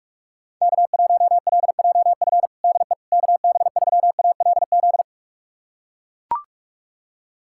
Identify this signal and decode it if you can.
Morse code, heard as a keyed tone, about 41 wpm, 700 Hz. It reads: K1LJF DE KB3ALZ
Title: USA Callsign DE Callsign 45wpm